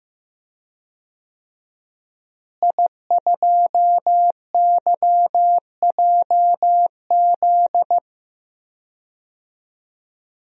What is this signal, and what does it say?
Morse code, heard as a keyed tone, about 15 wpm, 700 Hz. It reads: I2YJZ
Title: DXCC-100 Call Signs - 5 Characters - Repeated 3x - 15wpm